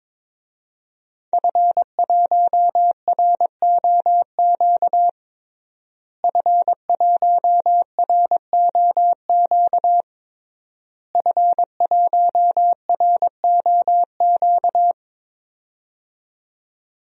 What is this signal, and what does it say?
Morse code, heard as a keyed tone, about 22 wpm, 700 Hz. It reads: F1ROQ F1ROQ F1ROQ
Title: DXCC-100 Call Signs - 5 Characters - Repeated 3x - 22wpm